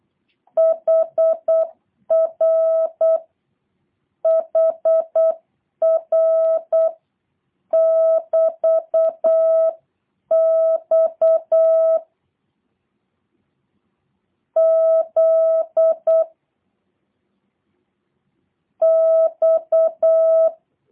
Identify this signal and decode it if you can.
Morse code, heard as a keyed tone, about 8 wpm, spaced as for 7 wpm, 650 Hz. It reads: HRHR<BT>X Z X